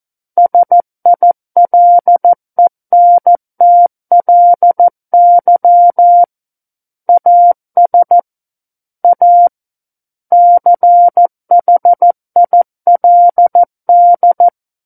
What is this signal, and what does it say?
Morse code, heard as a keyed tone, about 14 wpm, 700 Hz. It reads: SILENTLY AS A CHILD
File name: Book One / 05-Chapter 5 - 14 WPM